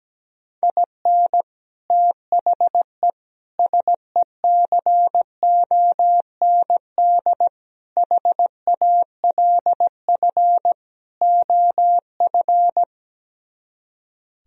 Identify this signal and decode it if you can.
Morse code, heard as a keyed tone, about 17 wpm, 700 Hz. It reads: IN THE SECOND HALF OF
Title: Speed-Racing - Sets of 5 Words 17wpm